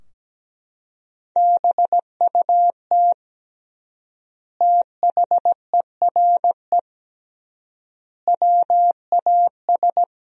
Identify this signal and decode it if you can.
Morse code, heard as a keyed tone, about 17 wpm, 700 Hz. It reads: BUT THERE WAS
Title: 3x Word-Spacing - Sentences from Top 100 Words - 17wpm